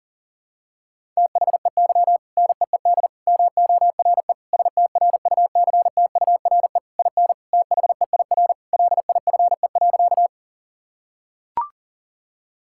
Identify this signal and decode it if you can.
Morse code, heard as a keyed tone, about 40 wpm, 700 Hz. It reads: THEY NEED MORE STRUCTURE IN THEIR LIFE.
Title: Sentences from Top 1000 Words - Review Part 6 - 40wpm